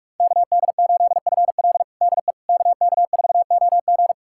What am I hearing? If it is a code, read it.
Morse code, heard as a keyed tone, about 46 wpm, 700 Hz. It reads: KD8FL DE KK4OG
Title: USA Callsign DE Callsign - 5 Characters - 45wpm